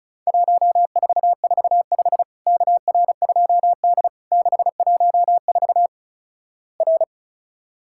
Morse code, heard as a keyed tone, about 36 wpm, 700 Hz. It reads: 1445 KR2D 614 R
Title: Worked All Europe DX Contest - QTC Exchange - 35wpm